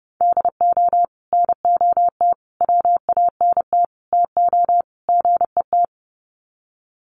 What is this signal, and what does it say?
Morse code, heard as a keyed tone, about 30 wpm, 700 Hz. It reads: DO NOT WANT TO GET
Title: Sets of 5 Words 30wpm